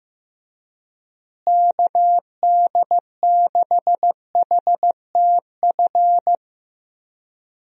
Morse code, heard as a keyed tone, about 15 wpm, 700 Hz. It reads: KD6HTF
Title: USA Call Signs - 6 Characters - Repeated 2x - 15wpm